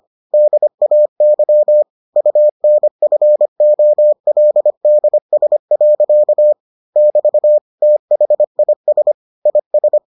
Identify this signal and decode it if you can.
Morse code, heard as a keyed tone, about 25 wpm, 600 Hz. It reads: DAY UNFOLDS. <BT> THIS IS